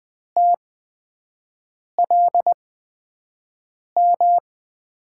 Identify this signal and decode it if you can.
Morse code, heard as a keyed tone, about 20 wpm, spaced as for 5 wpm, 700 Hz. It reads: TLM
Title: USA Call Signs - 8x Character Spacing - 20wpm